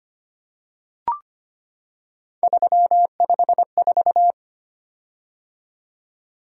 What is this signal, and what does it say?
Morse code, heard as a keyed tone, about 25 wpm, 700 Hz. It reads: 354
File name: Three Letter-Number Combinations - 25wpm